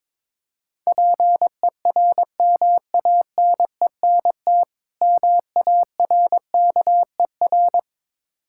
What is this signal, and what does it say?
Morse code, heard as a keyed tone, about 23 wpm, 690 Hz. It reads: PERMANENT MARKER
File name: Easy Words - Speed-Racing with Hints - 22wpm